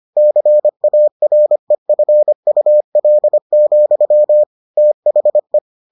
Morse code, heard as a keyed tone, about 25 wpm, 600 Hz. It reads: CAREFUL, THE